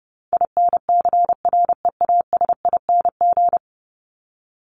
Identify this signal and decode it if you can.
Morse code, heard as a keyed tone, about 30 wpm, 700 Hz. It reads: INCREASING
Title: ING Suffix - Top 100 Words with Base Words 30wpm